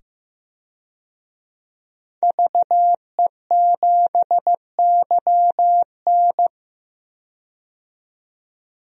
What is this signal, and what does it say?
Morse code, heard as a keyed tone, about 15 wpm, 700 Hz. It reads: VE7YN
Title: DXCC-100 Call Signs - 5 Characters - 15wpm